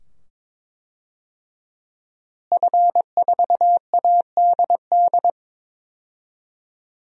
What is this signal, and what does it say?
Morse code, heard as a keyed tone, about 21 wpm, 700 Hz. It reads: F4ADD
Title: DXCC-100 Call Signs - 5 Characters - Repeated 2x - 22wpm